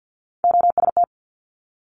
Morse code, heard as a keyed tone, about 50 wpm, 700 Hz. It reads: OST
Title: Multi-letter Phonemes Quick Fire 50wpm